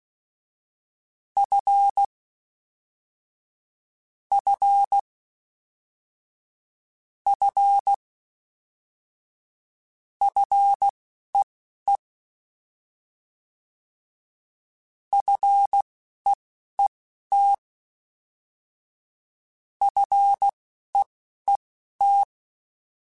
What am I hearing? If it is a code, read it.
Morse code, heard as a keyed tone, about 16 wpm, spaced as for 11 wpm, 800 Hz. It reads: F F F FEE FEET FEET
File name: F and L / F and L mp3 / F and L Prefixes 3x2 Twice @10